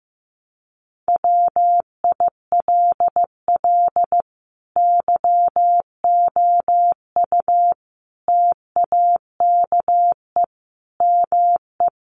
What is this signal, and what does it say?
Morse code, heard as a keyed tone, about 14 wpm, 700 Hz. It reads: WILL YOU TAKE ME
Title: Sentences from Top 200 Words 15wpm